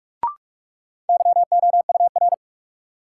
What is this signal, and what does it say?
Morse code, heard as a keyed tone, about 45 wpm, 700 Hz. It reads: YOUR
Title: Sets of 2 Words - New Word Drill - 45wpm